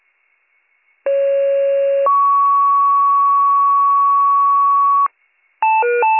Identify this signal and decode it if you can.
Two-tone page, heard as a one-second tone, then a three-second tone, about 550 then 1120 Hz. It is heard from 1.1 to 5.1 s.